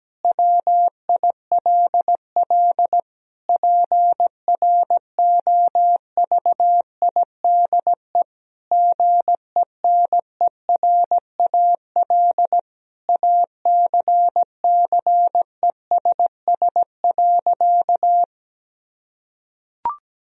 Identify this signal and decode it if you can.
Morse code, heard as a keyed tone, about 17 wpm, 700 Hz. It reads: WILL PROVIDE GENERAL ACCESS.
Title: Sentences from Top 1000 Words - Review Part 4 - 17wpm